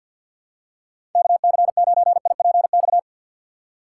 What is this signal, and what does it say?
Morse code, heard as a keyed tone, about 53 wpm, 700 Hz. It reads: KC9IPX
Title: USA Call Signs - 6x Character Spacing - 50wpm